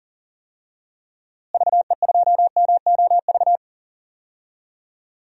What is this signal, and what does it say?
Morse code, heard as a keyed tone, about 37 wpm, 700 Hz. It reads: VE2MOV
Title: DXCC-100 Call Signs - 6 Characters - Repeated 3x - 40wpm